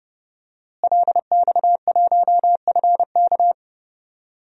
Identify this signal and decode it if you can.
Morse code, heard as a keyed tone, about 30 wpm, 700 Hz. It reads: LX1FK
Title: DXCC-100 Call Signs - 5 Characters - Repeated 2x - 30wpm